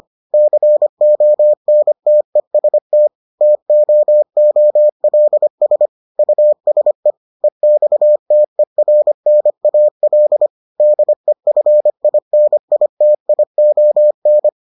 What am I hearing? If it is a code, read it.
Morse code, heard as a keyed tone, about 25 wpm, 600 Hz. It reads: CONTEST TOOLS USE EXTERNAL DEFINITION